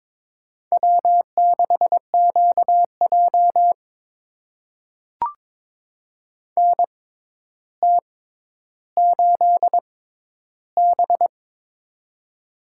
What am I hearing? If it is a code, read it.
Morse code, heard as a keyed tone, about 22 wpm, spaced as for 11 wpm, 700 Hz. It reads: W6QJ NT8B
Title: USA Call Signs - 4 Characters - 6x Character Spacing - 22wpm